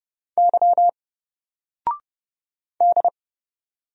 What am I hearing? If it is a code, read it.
Morse code, heard as a keyed tone, about 30 wpm, 700 Hz. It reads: Y D
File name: Single-Letter-Number-30wpm